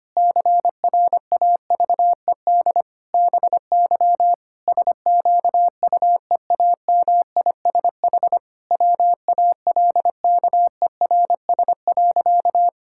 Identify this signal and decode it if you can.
Morse code, heard as a keyed tone, about 25 wpm, 700 Hz. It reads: CRAVED BY SQUEAMISH WALKERS.
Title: Pangram Challenge v2 - 25wpm